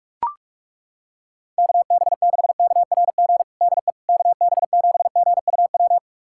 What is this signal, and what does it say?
Morse code, heard as a keyed tone, about 42 wpm, 700 Hz. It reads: KB6KRG DE KB7GUW
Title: USA Callsign DE Callsign - 6 Characters - 45wpm